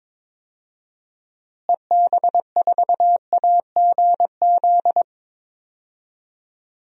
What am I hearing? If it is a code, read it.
Morse code, heard as a keyed tone, about 22 wpm, 700 Hz. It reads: EB4AGZ
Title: DXCC-100 Call Signs - 6 Characters - Repeated 3x - 22wpm